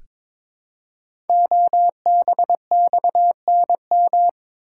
Morse code, heard as a keyed tone, about 22 wpm, 700 Hz. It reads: OBXNM
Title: Code Groups - 5 Letters - 22wpm